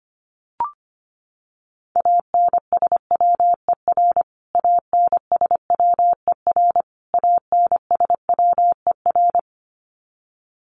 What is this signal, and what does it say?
Morse code, heard as a keyed tone, about 25 wpm, 700 Hz. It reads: ANSWER ANSWER ANSWER
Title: Top 500 Words Repeated 25wpm